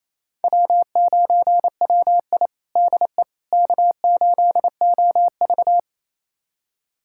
Morse code, heard as a keyed tone, about 28 wpm, 700 Hz. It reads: W9WI DE K8OV
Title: USA Callsign DE Callsign - 4 Characters - 28wpm